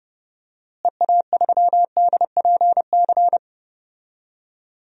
Morse code, heard as a keyed tone, about 30 wpm, 700 Hz. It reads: EA3DPC
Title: DXCC-100 Call Signs - 6 Characters - Repeated 3x - 30wpm